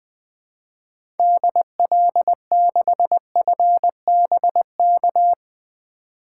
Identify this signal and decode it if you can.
Morse code, heard as a keyed tone, about 20 wpm, 700 Hz. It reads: DL6FBK
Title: 2023 SOTA Activator Call Signs - 20wpm